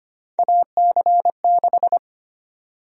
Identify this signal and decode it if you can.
Morse code, heard as a keyed tone, about 25 wpm, 700 Hz. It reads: AC6
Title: USA Call Sign Prefixes - 25wpm